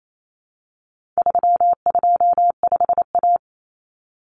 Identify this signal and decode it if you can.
Morse code, heard as a keyed tone, about 28 wpm, 700 Hz. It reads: 325A
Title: Common Cut Numbers - 4-Digit 28wpm